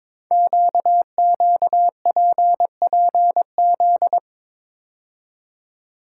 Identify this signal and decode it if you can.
Morse code, heard as a keyed tone, about 22 wpm, 700 Hz. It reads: QQPPZ
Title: Code Groups - 5 Letters - 22wpm